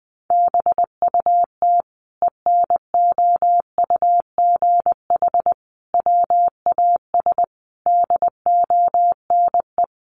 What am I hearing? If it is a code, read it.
Morse code, heard as a keyed tone, about 20 wpm, 700 Hz. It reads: BUT ENOUGH WAS DONE
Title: Sentences from Top 1100 Words 20wpm